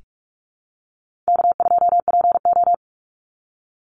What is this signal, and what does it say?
Morse code, heard as a keyed tone, about 44 wpm, 700 Hz. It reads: K2PO
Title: Call Signs Repeated 3x - 45wpm